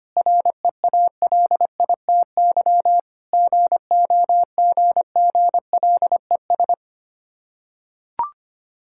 Morse code, heard as a keyed tone, about 25 wpm, 690 Hz. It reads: REALITY GOGGLES
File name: practice / 25wpm / Difficult Words with Hints - 25wpm